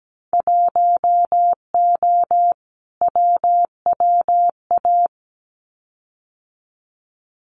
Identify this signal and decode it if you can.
Morse code, heard as a keyed tone, about 17 wpm, 700 Hz. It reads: 1O WWA
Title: Winter Field Day Exchanges 17wpm